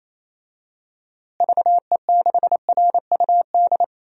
Morse code, heard as a keyed tone, about 28 wpm, 700 Hz. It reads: VE6RUD
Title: DXCC-100 Call Signs - 6 Characters - Repeated 3x - 28wpm